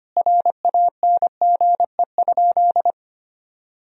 Morse code, heard as a keyed tone, about 25 wpm, 700 Hz. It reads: RANGE?
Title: Sentences from Top 1000 Words - Review Part 4 - 25wpm